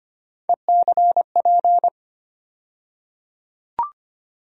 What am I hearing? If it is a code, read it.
Morse code, heard as a keyed tone, about 25 wpm, 700 Hz. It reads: ECP
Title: USA Call Sign Postfixes - 3 Letters 25wpm